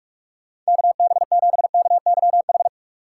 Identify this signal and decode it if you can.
Morse code, heard as a keyed tone, about 46 wpm, 700 Hz. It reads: KB7KYH
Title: USA Call Signs - 6 Characters - 2x Character Spacing - 45wpm